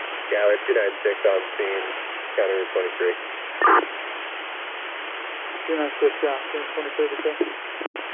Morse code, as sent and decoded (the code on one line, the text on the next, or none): none